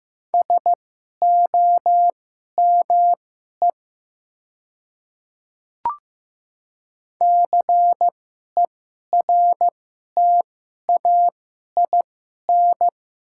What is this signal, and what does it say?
Morse code, heard as a keyed tone, about 15 wpm, spaced as for 11 wpm, 700 Hz. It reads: SOME CERTAIN
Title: 2x Character Spacing - Top 500 Words - 15wpm